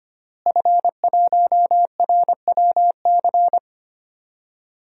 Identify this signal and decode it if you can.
Morse code, heard as a keyed tone, about 25 wpm, 700 Hz. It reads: F1RWC
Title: DXCC-100 Call Signs - 5 Characters - Repeated 3x - 25wpm